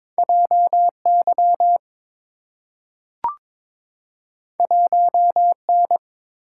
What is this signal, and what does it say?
Morse code, heard as a keyed tone, about 22 wpm, 690 Hz. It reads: JY 1N
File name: practice / 22wpm / Two Letter-Number Combinations - 22wpm